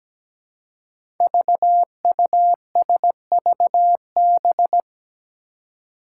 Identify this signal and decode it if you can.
Morse code, heard as a keyed tone, about 17 wpm, 700 Hz. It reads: VUSVB
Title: Five Challenging Letter-Number Combinations 17wpm